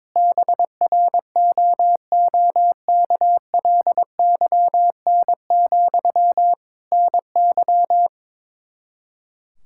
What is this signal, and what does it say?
Morse code, heard as a keyed tone, about 21 wpm, 700 Hz. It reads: BROOKLYN, NY
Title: Practice QSO - Exchange 1 - Beginning - 22wpm